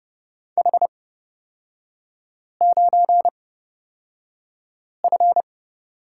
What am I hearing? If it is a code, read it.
Morse code, heard as a keyed tone, about 30 wpm, spaced as for 5 wpm, 700 Hz. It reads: H9F
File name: Lesson 36e - BT - Three Characters 30wpm